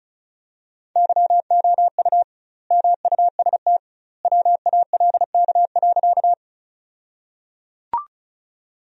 Morse code, heard as a keyed tone, about 36 wpm, 700 Hz. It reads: YOU MUST WALK.
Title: 2x Word-Spacing - Sentences from Top 300 Words - 35wpm